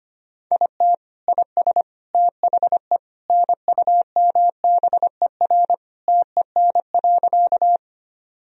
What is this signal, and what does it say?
Morse code, heard as a keyed tone, about 25 wpm, 700 Hz. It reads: IT IS THE NUMBER TEN.
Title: Sentences from Top 1000 Words - Review Part 2 - 25wpm